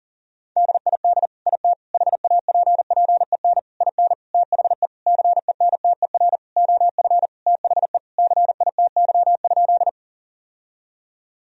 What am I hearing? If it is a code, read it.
Morse code, heard as a keyed tone, about 40 wpm, 700 Hz. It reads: DID IT HAPPEN IN THE CENTER OF THE CITY?
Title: Sentences from Top 1000 Words - Review Part 2 - 40wpm